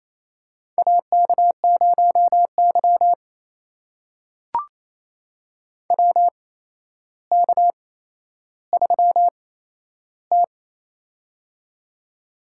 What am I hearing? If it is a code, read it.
Morse code, heard as a keyed tone, about 28 wpm, spaced as for 12 wpm, 700 Hz. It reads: AK0Y WK3T
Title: USA Call Signs - 4 Characters - 8x Character Spacing - 28wpm